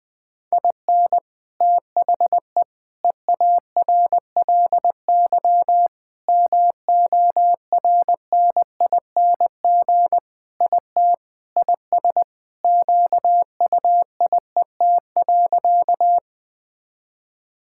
Morse code, heard as a keyed tone, about 20 wpm, 700 Hz. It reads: IN THE EARLY MORNING IT IS QUIET.
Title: Sentences with 3-Word Phrases - 20wpm